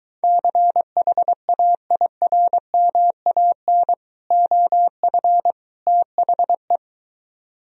Morse code, heard as a keyed tone, about 23 wpm, 700 Hz. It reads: CHAIRMAN OF THE